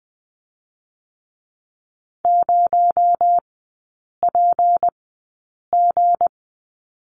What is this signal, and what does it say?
Morse code, heard as a keyed tone, about 20 wpm, spaced as for 8 wpm, 700 Hz. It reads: 0PG
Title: 218 Lesson 35e - Number 0 - Three Characters 20wpm